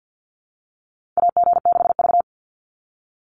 Morse code, heard as a keyed tone, about 50 wpm, 700 Hz. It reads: AG6V
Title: NA SOTA Chaser Callsigns 50wpm